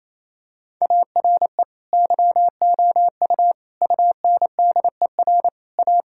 Morse code, heard as a keyed tone, about 28 wpm, 700 Hz. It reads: ARE YOU UNDER A
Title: Sentences from Top 1000 Words - Review Part 6 - 28wpm